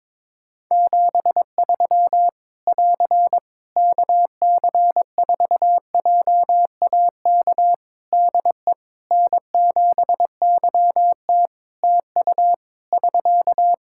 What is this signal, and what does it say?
Morse code, heard as a keyed tone, about 21 wpm, 700 Hz. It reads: 73 <AR> KC4JAK DE N7YT TU <SK>
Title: Practice QSO - Exchange 4 - Ending - 22wpm